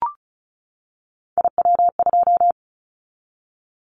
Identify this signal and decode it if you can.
Morse code, heard as a keyed tone, about 34 wpm, 700 Hz. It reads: IW2
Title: Callsign Prefixes for Top 25 DXCC - 35wpm